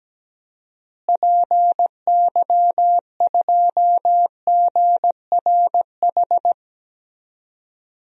Morse code, heard as a keyed tone, about 17 wpm, 700 Hz. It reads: PY2GRH
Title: DXCC-100 Call Signs - 6 Characters - Repeated 2x - 17wpm